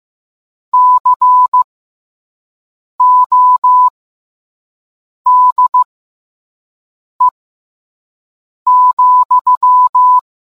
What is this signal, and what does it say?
Morse code, heard as a keyed tone, about 15 wpm, spaced as for 5 wpm, 1000 Hz. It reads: CODE,